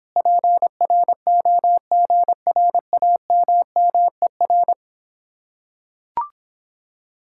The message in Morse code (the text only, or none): PROGRAMMER